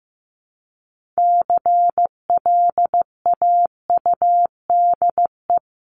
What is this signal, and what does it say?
Morse code, heard as a keyed tone, about 15 wpm, 700 Hz. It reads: CLAUDE
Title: CWT Exchange 15wpm